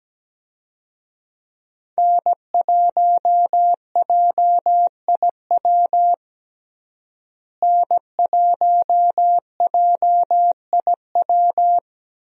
Morse code, heard as a keyed tone, about 17 wpm, 700 Hz. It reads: N1JIW N1JIW
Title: USA Call Signs - 5 Characters - Repeated 3x - 17wpm